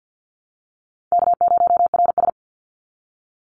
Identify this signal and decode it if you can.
Morse code, heard as a keyed tone, about 50 wpm, 700 Hz. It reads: K0RS
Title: NA SOTA Chaser Callsigns 50wpm